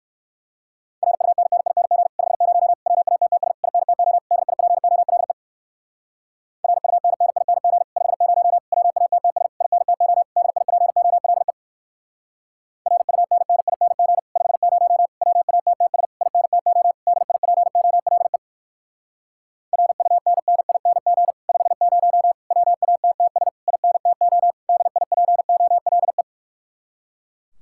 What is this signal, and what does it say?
Morse code, heard as a keyed tone, about 56 wpm, 700 Hz. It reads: RUNNING 50 WATTS INTO DIPOLE RUNNING 50 WATTS INTO DIPOLE RUNNING 50 WATTS INTO DIPOLE RUNNING 50 WATTS INTO DIPOLE
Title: Speed-Racing Common QSO Phrases - 45wpm